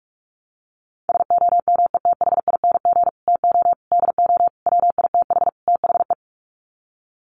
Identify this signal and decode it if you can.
Morse code, heard as a keyed tone, about 44 wpm, 700 Hz. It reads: SOMETHING TO DO WITH THE
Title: Sets of 5 Words Encore 45wpm